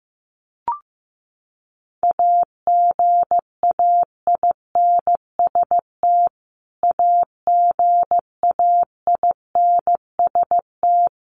Morse code, heard as a keyed tone, about 15 wpm, 700 Hz. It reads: AGAINST AGAINST
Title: Top 500 Words Repeated 15wpm